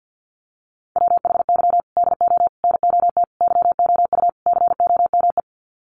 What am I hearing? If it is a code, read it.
Morse code, heard as a keyed tone, about 48 wpm, 700 Hz. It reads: WHY DO NOT YOU COME
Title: Sets of 5 Words Encore2 50wpm